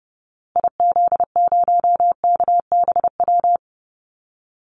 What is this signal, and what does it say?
Morse code, heard as a keyed tone, about 28 wpm, 700 Hz. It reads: IZ0KBW
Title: Call Signs Repeated 2x - 30wpm